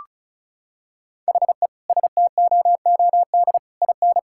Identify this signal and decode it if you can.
Morse code, heard as a keyed tone, about 36 wpm, 700 Hz. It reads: HE STOOD IN
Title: Sentences from Top 1000 Words - Review Part 6 - 35wpm